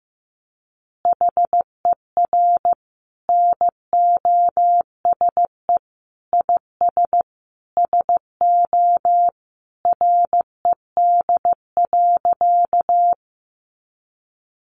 Morse code, heard as a keyed tone, about 15 wpm, 700 Hz. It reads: HER NOSE IS SO RED.
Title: Easy Phrases 15wpm